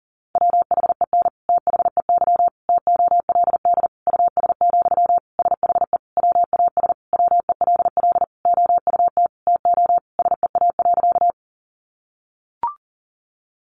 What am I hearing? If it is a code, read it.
Morse code, heard as a keyed tone, about 37 wpm, 700 Hz. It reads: WHEN THEY TOLD US, SHE WAS WELL OUT TO SEA.
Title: Sentences from Top 400 Words 40wpm